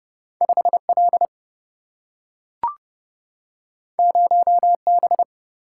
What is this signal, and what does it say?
Morse code, heard as a keyed tone, about 30 wpm, 700 Hz. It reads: 5 L 0 B
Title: Single-Letter Single-Number 30wpm